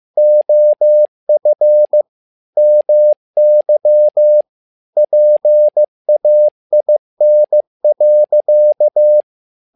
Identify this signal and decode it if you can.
Morse code, heard as a keyed tone, about 15 wpm, 600 Hz. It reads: OF MY PAIN.
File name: learning center / tower trouble / tower trouble 15wpm